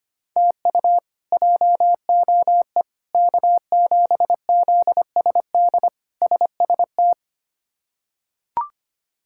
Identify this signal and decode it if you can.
Morse code, heard as a keyed tone, about 25 wpm, 700 Hz. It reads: TU JOE K7ZSD SST
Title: K1USN SST QSO Practice - 25wpm